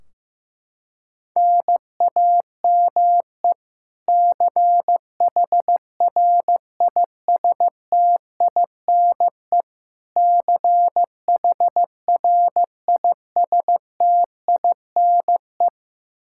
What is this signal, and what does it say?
Morse code, heard as a keyed tone, about 15 wpm, 700 Hz. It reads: NAME CHRISTINE CHRISTINE
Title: Practice QSO - Exchange 1 - Beginning - 15wpm